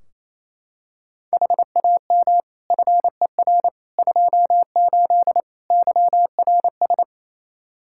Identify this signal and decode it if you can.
Morse code, heard as a keyed tone, about 28 wpm, 700 Hz. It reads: HAM FER 28 YRS
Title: Practice QSO - Exchange 3 - More Info - 28wpm